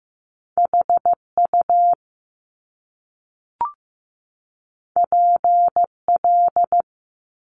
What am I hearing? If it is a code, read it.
Morse code, heard as a keyed tone, about 15 wpm, 700 Hz. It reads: HU PL
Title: Top Two Letter Combinations in Words 15wpm